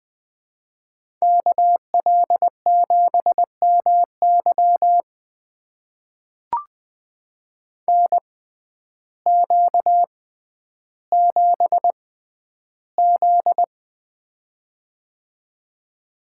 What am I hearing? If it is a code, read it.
Morse code, heard as a keyed tone, about 20 wpm, spaced as for 11 wpm, 700 Hz. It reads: KL7MY NQ7Z
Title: USA Call Signs - 6x Character Spacing - 20wpm